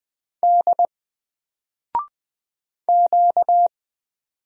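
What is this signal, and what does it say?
Morse code, heard as a keyed tone, about 20 wpm, 700 Hz. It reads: D Q
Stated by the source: Single letter and number Morse drill at 20 wpm, from N4LCD.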